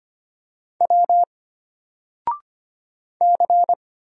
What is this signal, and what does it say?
Morse code, heard as a keyed tone, about 25 wpm, 700 Hz. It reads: W C
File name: Single-Letter-Number-25wpm